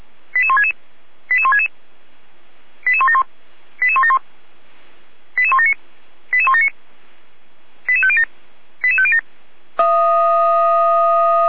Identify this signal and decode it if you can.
Five-tone selective call (ZVEI), heard as quick runs of five tones, tones of about 70 ms. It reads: 88166 88166 88171 88171 88179 88179 88507 88507